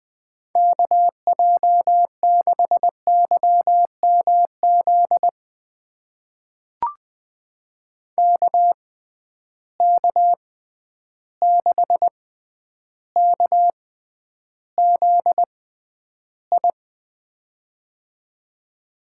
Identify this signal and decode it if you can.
Morse code, heard as a keyed tone, about 20 wpm, spaced as for 10 wpm, 700 Hz. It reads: KJ6YMZ KK6KZI
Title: USA Call Signs - 6 Characters - 6x Character Spacing - 20wpm